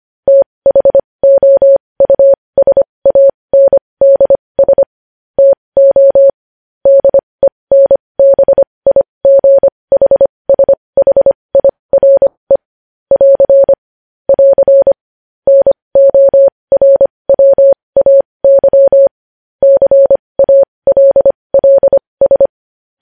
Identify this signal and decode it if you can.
Morse code, heard as a keyed tone, about 25 wpm, spaced as for 20 wpm, 550 Hz. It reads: THOUSANDS TO DENBIGHSHIRE <AR> <AR> NORWAY CALLS
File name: podcast / News headlines-1x-25x20